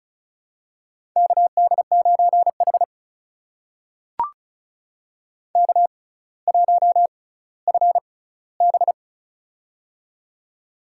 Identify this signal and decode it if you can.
Morse code, heard as a keyed tone, about 36 wpm, spaced as for 18 wpm, 700 Hz. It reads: KD9H K1FB
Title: USA Call Signs - 4 Characters - 6x Character Spacing - 35wpm